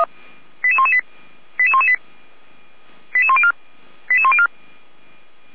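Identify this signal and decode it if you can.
Five-tone selective call (ZVEI), heard as quick runs of five tones, tones of about 70 ms. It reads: 88118 88118 88104 88104
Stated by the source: ZVEI five-tone selective call 88383 received by SDR.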